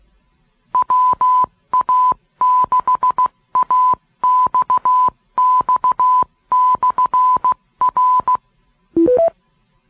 Morse code, heard as a keyed tone, about 16 wpm, 1000 Hz. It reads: WA6AXX/R